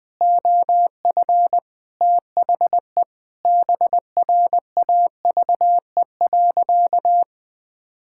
Morse code, heard as a keyed tone, about 21 wpm, 700 Hz. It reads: OF THE BRAVE.
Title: Sentences with 9-Word Phrases - 20wpm